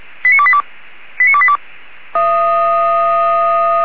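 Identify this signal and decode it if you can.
Five-tone selective call (ZVEI), heard as quick runs of five tones, tones of about 70 ms. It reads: 89282 89282
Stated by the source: ZVEI five-tone selective call received on SDR.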